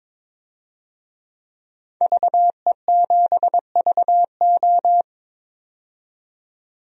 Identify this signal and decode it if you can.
Morse code, heard as a keyed tone, about 22 wpm, 700 Hz. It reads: VE7VO
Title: DXCC-100 Call Signs - 5 Characters - 22wpm